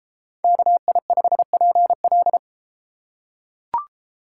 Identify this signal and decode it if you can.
Morse code, heard as a keyed tone, about 32 wpm, 700 Hz. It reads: KI5PL